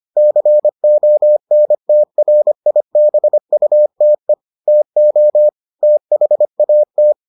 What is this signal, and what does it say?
Morse code, heard as a keyed tone, about 25 wpm, 600 Hz. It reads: CONTRIBUTE TO THAT